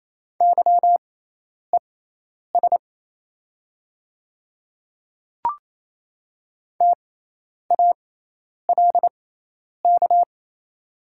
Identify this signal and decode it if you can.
Morse code, heard as a keyed tone, about 28 wpm, spaced as for 10 wpm, 690 Hz. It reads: YES TALK
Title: 6x Character Spacing - Top 500 Words - 28wpm